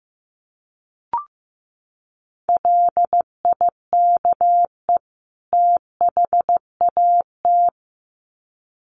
Morse code, heard as a keyed tone, about 15 wpm, 700 Hz. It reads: LIKE THAT
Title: Sets of 2 Words 15wpm